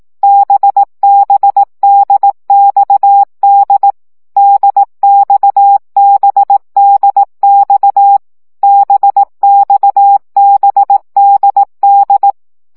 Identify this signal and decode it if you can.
Morse code, heard as a keyed tone, about 18 wpm, 800 Hz. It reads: BBDXD DXBDX BXBDD
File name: CW gyakorlas d b x 15 baud